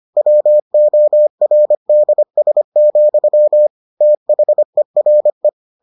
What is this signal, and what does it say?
Morse code, heard as a keyed tone, about 25 wpm, 600 Hz. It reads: WORDS, THERE